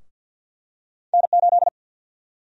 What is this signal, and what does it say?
Morse code, heard as a keyed tone, about 53 wpm, 700 Hz. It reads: N8
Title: USA Call Sign Prefixes - 50wpm